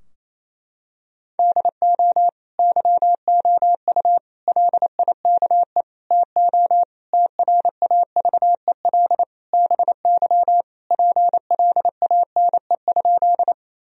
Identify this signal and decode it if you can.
Morse code, heard as a keyed tone, about 27 wpm, 700 Hz. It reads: DO YOU LIKE TO TRAVEL BY PLANE?
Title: Sentences from Top 1000 Words - Review Part 3 - 28wpm